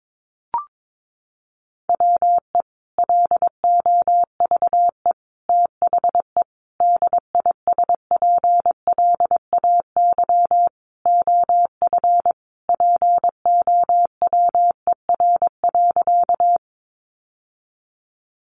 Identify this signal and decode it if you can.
Morse code, heard as a keyed tone, about 22 wpm, 700 Hz. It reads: WE LOVE THE DISPLAY OF POWER.
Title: Sentences from Top 1200 Words 22wpm